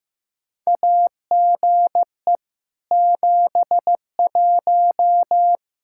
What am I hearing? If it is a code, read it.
Morse code, heard as a keyed tone, about 15 wpm, 700 Hz. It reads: AGE 71
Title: Practice QSO - Exchange 3 - More Info - 15wpm